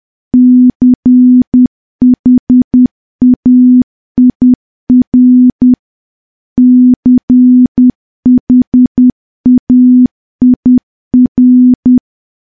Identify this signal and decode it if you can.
Morse code, heard as a keyed tone, about 10 wpm, 250 Hz. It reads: CHAIR CHAIR
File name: morse / morse-training-10WPM